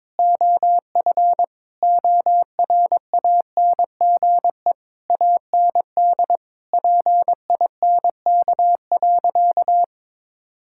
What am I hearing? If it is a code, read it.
Morse code, heard as a keyed tone, about 22 wpm, 700 Hz. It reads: OF ORANGE AND PINK.
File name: Difficult Sentences with Hints - 22wpm